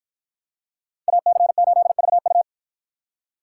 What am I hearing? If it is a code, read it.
Morse code, heard as a keyed tone, about 53 wpm, 700 Hz. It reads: AC8VU